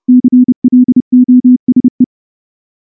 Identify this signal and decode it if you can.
Morse code, heard as a keyed tone, about 29 wpm, 250 Hz. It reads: CLOSE
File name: morse / morse-training-30WPM